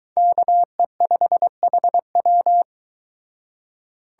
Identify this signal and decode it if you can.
Morse code, heard as a keyed tone, about 22 wpm, 700 Hz. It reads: KE5HW